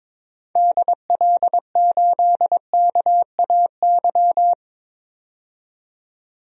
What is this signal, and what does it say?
Morse code, heard as a keyed tone, about 22 wpm, 700 Hz. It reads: DL8KAY
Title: DXCC-100 Call Signs - 6 Characters - 22wpm